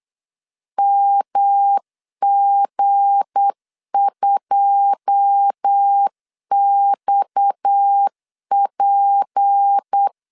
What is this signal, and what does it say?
Morse code, heard as a keyed tone, about 8 wpm, 800 Hz. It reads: MG2XP
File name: LEZIONI / 9 28 8wpm 3512